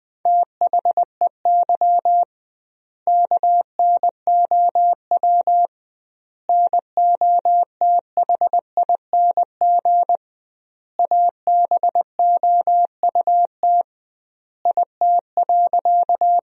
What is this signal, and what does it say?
Morse code, heard as a keyed tone, about 20 wpm, 700 Hz. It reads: THEY KNOW NOTHING ABOUT IT.
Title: 2x Word-Spacing - Sentences from Top 500 Words - 20wpm